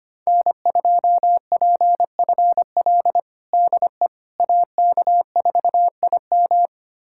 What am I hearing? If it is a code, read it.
Morse code, heard as a keyed tone, about 25 wpm, 700 Hz. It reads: N2PFL DE AK4IM
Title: USA Callsign DE Callsign - 5 Characters - 25wpm